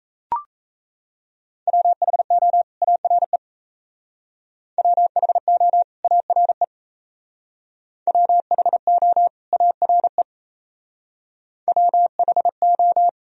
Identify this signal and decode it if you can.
Morse code, heard as a keyed tone, about 36 wpm, 700 Hz. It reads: WHO ARE WHO ARE WHO ARE WHO